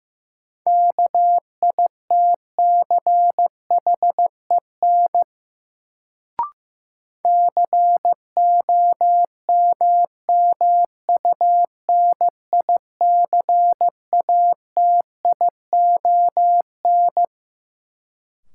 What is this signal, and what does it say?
Morse code, heard as a keyed tone, about 15 wpm, 700 Hz. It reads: KITCHEN COMMUNICATION
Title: Sentences from Top 1400 Words - New Word Drill Alt - 15wpm